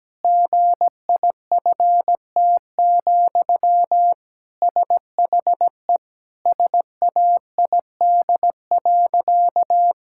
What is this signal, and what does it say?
Morse code, heard as a keyed tone, about 17 wpm, 700 Hz. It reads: GIFT, SHE SAID.
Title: Sentences from Top 1000 Words - Review Part 6 - 17wpm